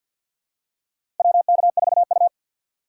Morse code, heard as a keyed tone, about 52 wpm, 700 Hz. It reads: WK4U
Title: USA Call Signs - 2x Character Spacing - 50wpm